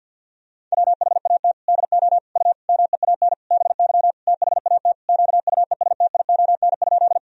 Morse code, heard as a keyed tone, about 50 wpm, 700 Hz. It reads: WHAT DO U MEAN BY THAT QUESTION?
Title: Sentences from Top 400 Words and Abbreviations 50wpm